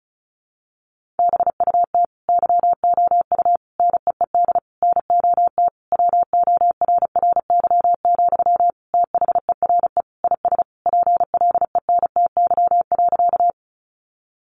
Morse code, heard as a keyed tone, about 34 wpm, 700 Hz. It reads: BUT YOU NEED NOT WORRY, THERE IS PLENTY.
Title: Sentences from Top 1300 Words 35wpm